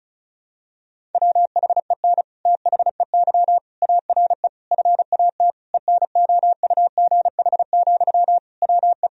Morse code, heard as a keyed tone, about 36 wpm, 700 Hz. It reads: WHEN THEY ARE FAT ENOUGH, WE
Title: Sentences from Top 1000 Words - Review Part 4 - 35wpm